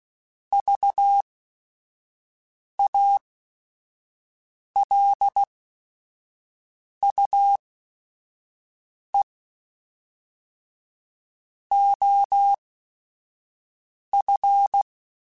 Morse code, heard as a keyed tone, about 16 wpm, spaced as for 5 wpm, 800 Hz. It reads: VALUE OF
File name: Technical Subjects / Technical Subjects-mp3s / XL @5